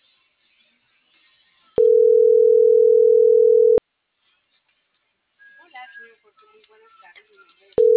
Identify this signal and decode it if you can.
Ringback tone (North America), 440 + 480 Hz.